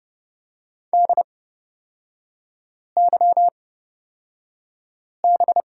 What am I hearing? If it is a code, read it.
Morse code, heard as a keyed tone, about 30 wpm, spaced as for 5 wpm, 700 Hz. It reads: DYB
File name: Lesson 37e - Letter x - Three Characters 30wpm